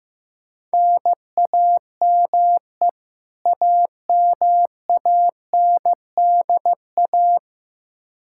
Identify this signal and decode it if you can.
Morse code, heard as a keyed tone, about 15 wpm, 700 Hz. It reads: NAME AMANDA
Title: Practice QSO - Exchange 1 - Beginning - 15wpm